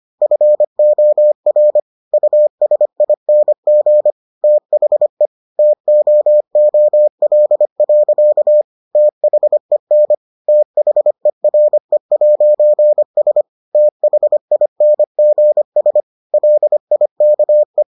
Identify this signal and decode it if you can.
Morse code, heard as a keyed tone, about 25 wpm, 600 Hz. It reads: FOR USING THE TOOL. THEN THERE'S THINGS LIKE